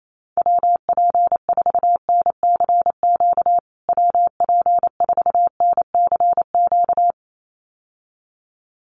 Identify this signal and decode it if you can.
Morse code, heard as a keyed tone, about 28 wpm, 700 Hz. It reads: WP4NCQ WP4NCQ
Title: USA Call Signs Repeated 2x 28wpm